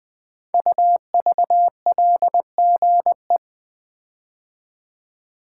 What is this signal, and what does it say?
Morse code, heard as a keyed tone, about 20 wpm, 700 Hz. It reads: UVLGE
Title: Code Groups - 5 Letter:Numbers - 20wpm